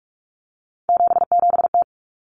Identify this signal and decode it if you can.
Morse code, heard as a keyed tone, about 44 wpm, 700 Hz. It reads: 77T
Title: Common Cut Numbers - 3-Digit 45wpm